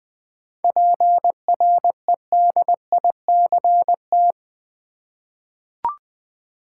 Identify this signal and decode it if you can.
Morse code, heard as a keyed tone, about 20 wpm, 700 Hz. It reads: PREDICT
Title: 7 Letter Words - First 3 Letters Given - 20wpm